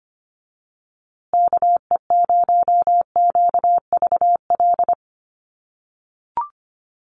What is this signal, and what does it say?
Morse code, heard as a keyed tone, about 24 wpm, 700 Hz. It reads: KE0QVL
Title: USA Call Signs Repeated 2x 25wpm